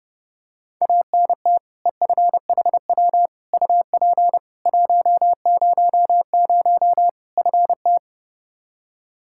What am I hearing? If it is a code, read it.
Morse code, heard as a keyed tone, about 30 wpm, 700 Hz. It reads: ANT EFHW UP 100 FT
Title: Practice QSO - Exchange 2 - Info - 30wpm